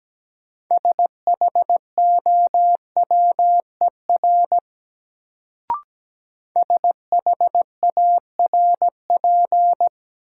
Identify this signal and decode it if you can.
Morse code, heard as a keyed tone, about 17 wpm, 700 Hz. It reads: SHOWER SHARP
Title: Sentences from Top 1600 Words - New Word Drill Alt - 17wpm